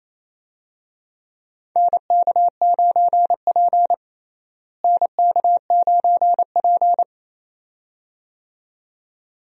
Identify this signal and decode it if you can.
Morse code, heard as a keyed tone, about 28 wpm, 700 Hz. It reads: NK9P NK9P
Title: USA Call Signs - 4 Characters - Repeated 2x - 28wpm